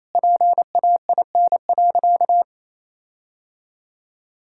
Morse code, heard as a keyed tone, about 28 wpm, 700 Hz. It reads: PAIN.
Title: Sentences from Top 1600 Words 28wpm